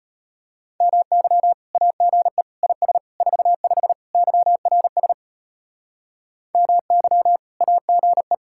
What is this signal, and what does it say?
Morse code, heard as a keyed tone, about 30 wpm, 700 Hz. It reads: MY AGE IS 45 YRS MY AGE
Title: Speed-Racing Common QSO Phrases Encore - 25wpm